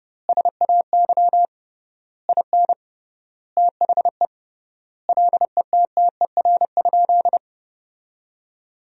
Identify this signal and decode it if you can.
Morse code, heard as a keyed tone, about 30 wpm, 700 Hz. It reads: SAY IN THE LETTER?
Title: 3x Word-Spacing - Sentences from Top 300 Words - 30wpm